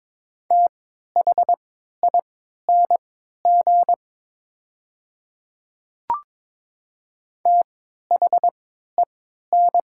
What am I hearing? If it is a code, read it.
Morse code, heard as a keyed tone, about 22 wpm, spaced as for 12 wpm, 700 Hz. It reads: THING THEN
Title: 3x Character Spacing - Top 200 Words - 22wpm